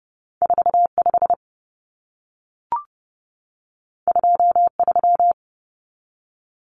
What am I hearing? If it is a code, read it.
Morse code, heard as a keyed tone, about 30 wpm, 700 Hz. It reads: 45 23
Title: Numbers - 2-Digit 30wpm